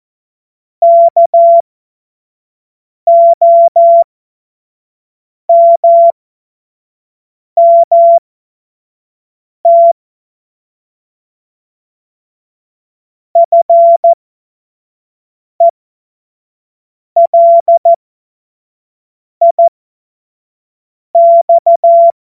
Morse code, heard as a keyed tone, about 14 wpm, spaced as for 5 wpm, 680 Hz. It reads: KOMMT FELIX
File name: Morsekurs / Witze / Witz 14. Februar 2023 25 F70